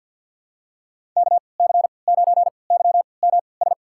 Morse code, heard as a keyed tone, about 51 wpm, spaced as for 28 wpm, 700 Hz. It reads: KC9YMS
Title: USA Call Signs - 6 Characters - 3x Character Spacing - 50wpm